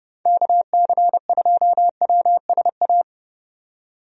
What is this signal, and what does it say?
Morse code, heard as a keyed tone, about 30 wpm, 700 Hz. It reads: KC2WSA
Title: USA Call Signs - 6 Characters - 30wpm